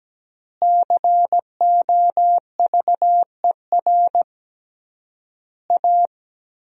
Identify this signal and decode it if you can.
Morse code, heard as a keyed tone, about 17 wpm, 700 Hz. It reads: COVER A
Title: 3x Word-Spacing - Sentences from Top 500 Words - 17wpm